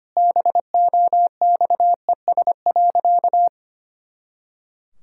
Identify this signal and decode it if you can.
Morse code, heard as a keyed tone, about 25 wpm, 700 Hz. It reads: BOXES.
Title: Pangram Challenge v2 - 25wpm